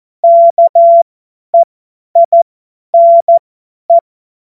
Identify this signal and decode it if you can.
Morse code, heard as a keyed tone, about 14 wpm, spaced as for 10 wpm, 680 Hz. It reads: KEINE
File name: Morsekurs / Witze / Witz 27. Dezember 2022 50 F70